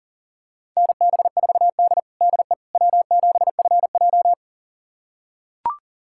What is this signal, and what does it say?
Morse code, heard as a keyed tone, about 40 wpm, 700 Hz. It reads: NB4D DE W7FJ
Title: USA Callsign DE Callsign - 4 Characters - 40wpm